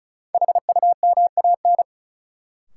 Morse code, heard as a keyed tone, about 36 wpm, 700 Hz. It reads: HUMAN
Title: Sentences from Top 600 Words - New Word Drill - 35wpm